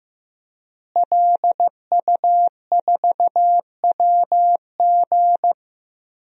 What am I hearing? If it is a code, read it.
Morse code, heard as a keyed tone, about 15 wpm, 700 Hz. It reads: LU4WG
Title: DXCC-100 Call Signs - 5 Characters - Repeated 3x - 15wpm